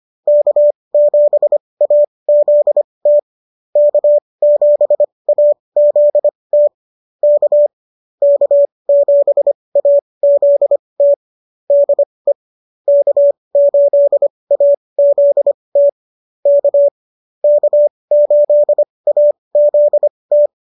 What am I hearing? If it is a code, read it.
Morse code, heard as a keyed tone, about 25 wpm, spaced as for 20 wpm, 590 Hz. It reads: K7AZT K7AZT K K7AZT DE K8AZT K K8AZT